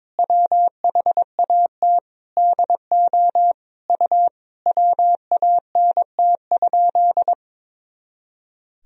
Morse code, heard as a keyed tone, about 22 wpm, 700 Hz. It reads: WHAT DO U WANT?
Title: Sentences from Top 200 Words and Abbreviations 22wpm